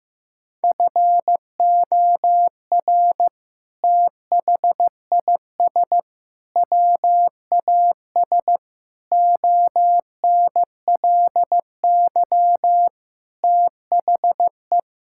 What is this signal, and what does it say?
Morse code, heard as a keyed tone, about 15 wpm, 700 Hz. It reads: FOR THIS WAS ONLY THE
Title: Sentences from Top 1000 Words - Review Part 5 - 15wpm